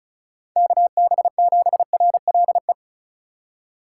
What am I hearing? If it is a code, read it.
Morse code, heard as a keyed tone, about 36 wpm, 700 Hz. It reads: KB7RLE
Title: USA Call Signs - 4x Character Spacing - 35wpm